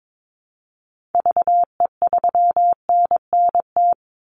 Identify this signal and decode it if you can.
Morse code, heard as a keyed tone, about 22 wpm, 700 Hz. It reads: VE3NNT
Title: Call Signs Repeated 3x - 22wpm